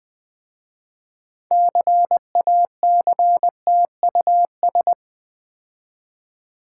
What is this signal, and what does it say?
Morse code, heard as a keyed tone, about 20 wpm, 700 Hz. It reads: CACTUS
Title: Difficult Words with Hints - 20wpm